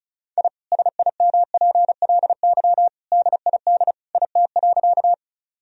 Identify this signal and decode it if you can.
Morse code, heard as a keyed tone, about 36 wpm, 700 Hz. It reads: I SIMPLY DID IT.
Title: Sentences from Top 1000 Words - Review Part 4 - 35wpm